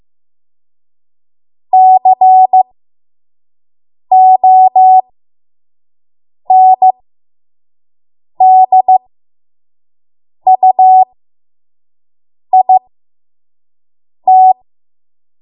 Morse code, heard as a keyed tone, about 15 wpm, spaced as for 5 wpm, 750 Hz. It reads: CONDUIT
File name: Morse / 05WPM